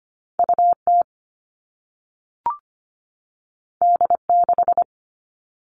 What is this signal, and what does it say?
Morse code, heard as a keyed tone, about 25 wpm, 700 Hz. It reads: U T D 6
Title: All Cut Numbers - 2-Digit 25wpm